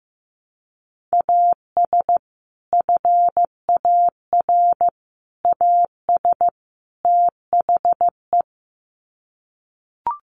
Morse code, heard as a keyed tone, about 15 wpm, 700 Hz. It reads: AS FAR AS THE
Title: Sets of 4 Words 15wpm